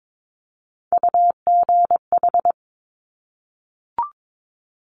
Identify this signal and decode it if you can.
Morse code, heard as a keyed tone, about 22 wpm, 700 Hz. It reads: UGH
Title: Top Three Letter Combinations in Words 22wpm